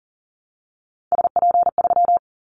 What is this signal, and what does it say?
Morse code, heard as a keyed tone, about 38 wpm, 700 Hz. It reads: SP3
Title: Callsign Prefixes for Top 25 DXCC - 40wpm